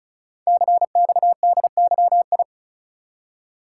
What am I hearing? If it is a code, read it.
Morse code, heard as a keyed tone, about 36 wpm, 700 Hz. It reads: CXDYI
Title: Code Groups - 5 Letters - 35wpm